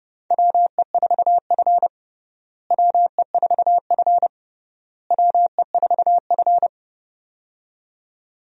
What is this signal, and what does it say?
Morse code, heard as a keyed tone, about 30 wpm, 700 Hz. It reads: WE4F WE4F WE4F
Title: USA Call Signs - 4 Characters - Repeated 3x - 30wpm